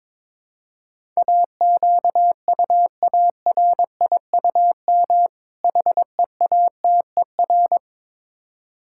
Morse code, heard as a keyed tone, about 22 wpm, 700 Hz. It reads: AQUARIUM HEATER
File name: Difficult Words with Hints - 22wpm